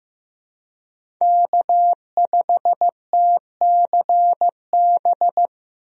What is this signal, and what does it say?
Morse code, heard as a keyed tone, about 15 wpm, 700 Hz. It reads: K5TCB
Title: USA Call Signs - 5 Characters - Repeated 2x - 15wpm